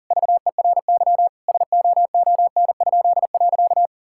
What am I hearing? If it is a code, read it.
Morse code, heard as a keyed tone, about 37 wpm, 700 Hz. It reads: VERY SOON?.
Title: Pangram Challenge v2 - 40wpm